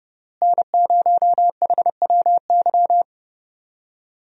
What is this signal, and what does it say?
Morse code, heard as a keyed tone, about 32 wpm, 700 Hz. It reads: N0HWY
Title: USA Call Signs - 5 Characters - Repeated 3x - 30wpm